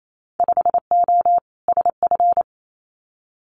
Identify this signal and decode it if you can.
Morse code, heard as a keyed tone, about 28 wpm, 700 Hz. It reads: 5O SF
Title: Winter Field Day Exchanges Encore 28wpm